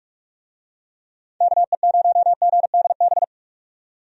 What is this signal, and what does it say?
Morse code, heard as a keyed tone, about 46 wpm, 700 Hz. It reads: KE0GDB
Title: USA Call Signs - 6 Characters - 45wpm